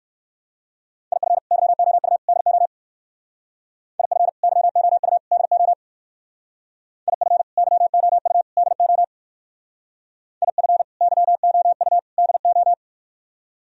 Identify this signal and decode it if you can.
Morse code, heard as a keyed tone, about 58 wpm, 700 Hz. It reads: IF YOU DO IF YOU DO IF YOU DO IF YOU DO